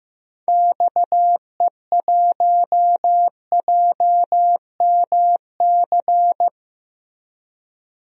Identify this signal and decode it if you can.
Morse code, heard as a keyed tone, about 15 wpm, 700 Hz. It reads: XE1JMC
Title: DXCC-100 Call Signs - 6 Characters - Repeated 2x - 15wpm